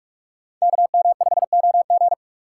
Morse code, heard as a keyed tone, about 42 wpm, 700 Hz. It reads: KM5OG